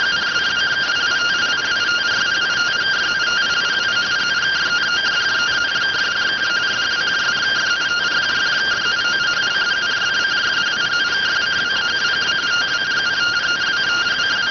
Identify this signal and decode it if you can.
Two-tone teleprinter signal, about 170 Hz shift.